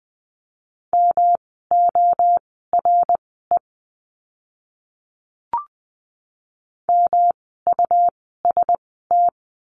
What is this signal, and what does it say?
Morse code, heard as a keyed tone, about 20 wpm, spaced as for 14 wpm, 700 Hz. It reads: MORE MUST
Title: 237 Lesson 38f - Letter q - Words 20wpm